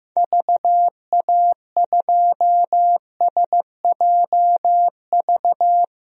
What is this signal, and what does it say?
Morse code, heard as a keyed tone, about 15 wpm, 700 Hz. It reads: VA2SJV
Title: DXCC-100 Call Signs - 6 Characters - Repeated 2x - 15wpm